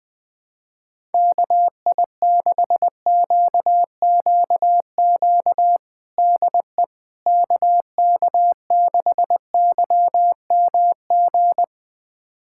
Morse code, heard as a keyed tone, about 21 wpm, 700 Hz. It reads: KI6QQQ DE KK6YMG
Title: USA Callsign DE Callsign 20wpm